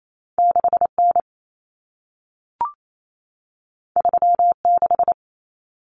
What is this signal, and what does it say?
Morse code, heard as a keyed tone, about 28 wpm, 700 Hz. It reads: 6N 36
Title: Common Cut Numbers - 2-Digit 28wpm